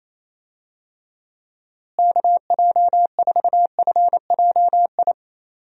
Morse code, heard as a keyed tone, about 27 wpm, 700 Hz. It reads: KJ4FJI
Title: USA Call Signs - 6 Characters - Repeated 2x - 28wpm